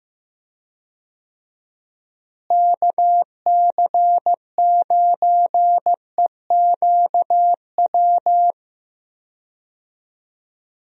Morse code, heard as a keyed tone, about 15 wpm, 700 Hz. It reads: KC9EQW
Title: USA Call Signs - 6 Characters - Repeated 2x - 15wpm